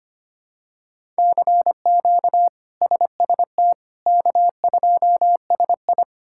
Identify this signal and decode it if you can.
Morse code, heard as a keyed tone, about 25 wpm, 700 Hz. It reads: CQ SST K2SI
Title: K1USN SST QSO Practice - 25wpm